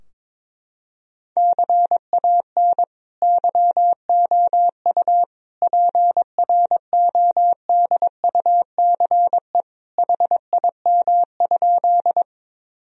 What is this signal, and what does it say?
Morse code, heard as a keyed tone, about 21 wpm, 700 Hz. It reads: CAN YOU PRODUCE HIM?
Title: Sentences from Top 1000 Words - Review Part 3 - 22wpm